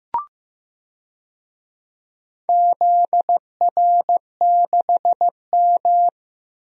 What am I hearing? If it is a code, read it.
Morse code, heard as a keyed tone, about 15 wpm, 700 Hz. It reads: ZR6M
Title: DXCC-100 Call Signs - 4 Characters - Repeated 2x - 15wpm